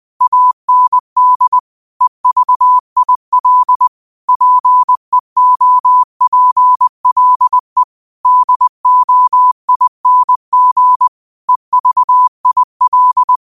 Morse code, heard as a keyed tone, about 20 wpm, 1000 Hz. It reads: AND EVIL PEOPLE DOING EVIL
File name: qotdfiles / 20wpm